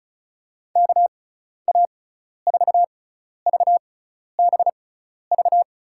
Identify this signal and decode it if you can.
Morse code, heard as a keyed tone, about 34 wpm, spaced as for 12 wpm, 700 Hz. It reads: KA4VBV
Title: USA Call Signs - 6 Characters - 6x Character Spacing - 35wpm